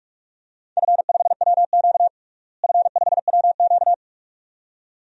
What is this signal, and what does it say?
Morse code, heard as a keyed tone, about 42 wpm, 700 Hz. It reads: F5WQ F5WQ
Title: DXCC-100 Call Signs - 4 Characters - Repeated 2x - 45wpm